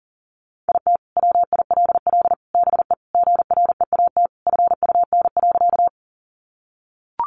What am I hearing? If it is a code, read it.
Morse code, heard as a keyed tone, about 40 wpm, 700 Hz. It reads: IT WILL BE GREAT FUN.
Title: Sentences from Top 700 Words 40wpm